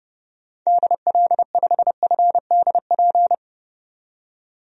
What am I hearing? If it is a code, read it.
Morse code, heard as a keyed tone, about 30 wpm, 700 Hz. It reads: DL5FDP
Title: DXCC-100 Call Signs - 6 Characters - 30wpm